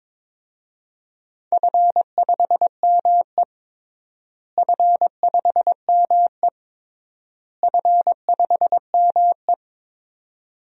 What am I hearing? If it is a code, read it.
Morse code, heard as a keyed tone, about 22 wpm, 700 Hz. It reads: F5ME F5ME F5ME
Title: DXCC-100 Call Signs - 4 Characters - Repeated 3x - 22wpm